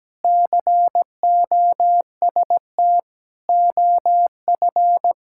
Morse code, heard as a keyed tone, about 17 wpm, 700 Hz. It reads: COST OF
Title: Sentences from Top 1000 Words - Review Part 5 - 17wpm